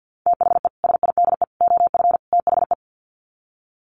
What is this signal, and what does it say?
Morse code, heard as a keyed tone, about 50 wpm, 700 Hz. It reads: THE SIDE OF THE
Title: Sets of 4 Words 50wpm